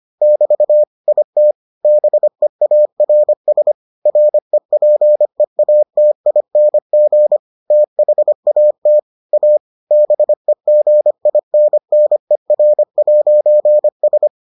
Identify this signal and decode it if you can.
Morse code, heard as a keyed tone, about 25 wpm, 600 Hz. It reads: <BT> IT BEARS REPEATING THAT A BEGINNER'S